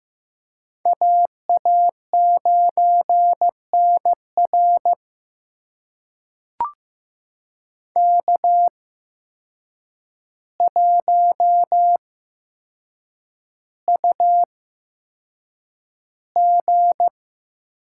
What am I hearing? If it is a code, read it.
Morse code, heard as a keyed tone, about 15 wpm, spaced as for 7 wpm, 700 Hz. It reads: AA9NR K1UG
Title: USA Call Signs - 5 Characters - 8x Character Spacing - 15wpm